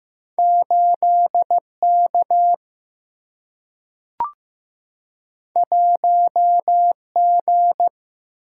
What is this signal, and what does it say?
Morse code, heard as a keyed tone, about 15 wpm, 700 Hz. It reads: 8 K 1 G
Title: Single-Letter Single-Number 15wpm